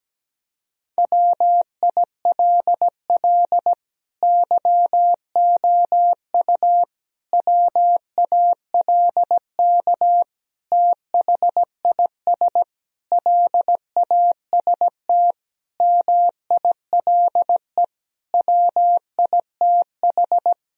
Morse code, heard as a keyed tone, about 17 wpm, 700 Hz. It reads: WILL YOU WALK THIS LAST MILE WITH
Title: Sentences from Top 1000 Words - Review Part 1 - 17wpm